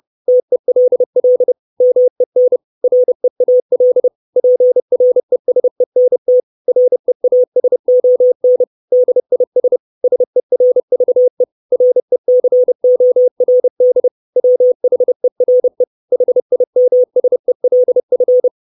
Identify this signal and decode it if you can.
Morse code, heard as a keyed tone, about 30 wpm, 500 Hz. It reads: TELL MEN REAL PRESENT REASON DIS SERVE RECORD WHERE HIMSELF
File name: 632 Words 30WPM